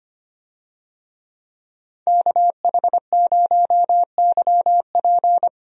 Morse code, heard as a keyed tone, about 25 wpm, 700 Hz. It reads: KH0YP
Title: USA Call Signs - 5 Characters - 25wpm